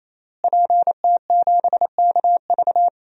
Morse code, heard as a keyed tone, about 28 wpm, 700 Hz. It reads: PT7KV
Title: DXCC-100 Call Signs - 5 Characters - Repeated 3x - 28wpm